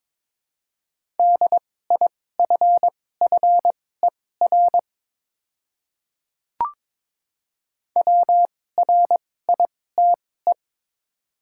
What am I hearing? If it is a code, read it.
Morse code, heard as a keyed tone, about 22 wpm, spaced as for 16 wpm, 700 Hz. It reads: DIFFER WRITE
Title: 2x Character Spacing - Top 200 Words - 22wpm